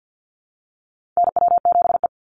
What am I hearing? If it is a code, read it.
Morse code, heard as a keyed tone, about 50 wpm, 700 Hz. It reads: NW7E
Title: NA SOTA Chaser Callsigns 50wpm